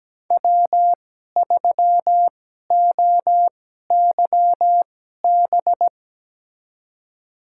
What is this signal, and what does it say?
Morse code, heard as a keyed tone, about 17 wpm, spaced as for 12 wpm, 700 Hz. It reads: W3OYB
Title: USA Call Signs - 5 Characters - 2x Character Spacing - 17wpm